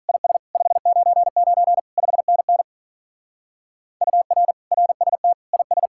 Morse code, heard as a keyed tone, about 44 wpm, 700 Hz. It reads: IS 599 5NN UR RST IS